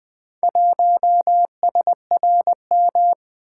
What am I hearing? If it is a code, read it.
Morse code, heard as a keyed tone, about 20 wpm, 700 Hz. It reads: 1SRM
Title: Four Letter-Number Combinations - 20wpm